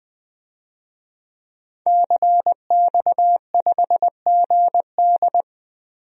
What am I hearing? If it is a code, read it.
Morse code, heard as a keyed tone, about 20 wpm, 700 Hz. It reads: CX5GD
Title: DXCC-100 Call Signs - 5 Characters - Repeated 2x - 20wpm